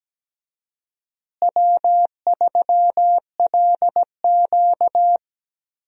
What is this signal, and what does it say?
Morse code, heard as a keyed tone, about 17 wpm, 700 Hz. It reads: W3LQ
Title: USA Call Signs - 4 Characters - Repeated 2x - 17wpm